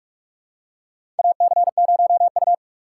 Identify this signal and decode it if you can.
Morse code, heard as a keyed tone, about 41 wpm, 700 Hz. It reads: AC0U
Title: USA Call Signs - 4 Characters - 3x Character Spacing - 45wpm